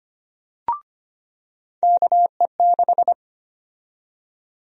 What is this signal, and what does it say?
Morse code, heard as a keyed tone, about 25 wpm, 700 Hz. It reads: KE6
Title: USA Call Sign Prefixes - 25wpm